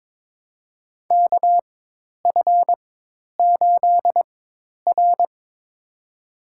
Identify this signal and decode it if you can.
Morse code, heard as a keyed tone, about 22 wpm, spaced as for 10 wpm, 700 Hz. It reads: KF8R
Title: USA Call Signs - 4 Characters - 4x Character Spacing - 22wpm